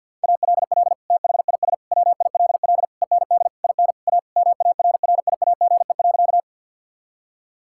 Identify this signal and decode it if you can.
Morse code, heard as a keyed tone, about 50 wpm, 690 Hz. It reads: ALL THIS WILL END IN A MARRIAGE.
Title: Sentences from Top 1000 Words - Review Part 5 - 50wpm